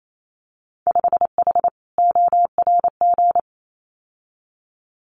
Morse code, heard as a keyed tone, about 28 wpm, 700 Hz. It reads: 5H ORG
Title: Winter Field Day Exchanges Encore 28wpm